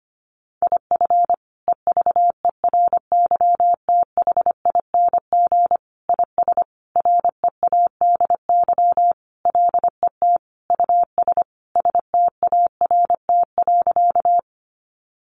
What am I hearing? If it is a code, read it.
Morse code, heard as a keyed tone, about 25 wpm, 700 Hz. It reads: IF EVERYTHING IS READY LET US START.
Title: Sentences from Top 700 Words 25wpm